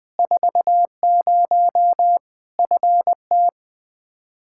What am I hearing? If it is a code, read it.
Morse code, heard as a keyed tone, about 20 wpm, 700 Hz. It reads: 40 FT
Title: Practice QSO - Exchange 2 - Info - 20wpm